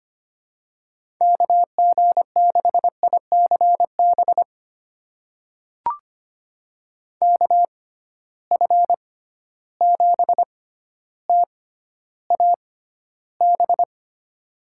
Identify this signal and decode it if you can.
Morse code, heard as a keyed tone, about 25 wpm, spaced as for 13 wpm, 700 Hz. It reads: KG6ICB KF7TAB
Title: USA Call Signs - 6 Characters - 6x Character Spacing - 25wpm